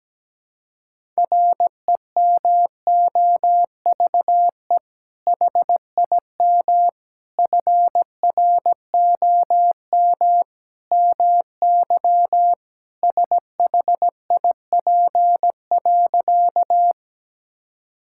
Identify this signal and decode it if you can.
Morse code, heard as a keyed tone, about 17 wpm, 700 Hz. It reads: REMOVE HIM FROM MY SHIP.
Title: Sentences from Top 1000 Words - Review Part 5 - 17wpm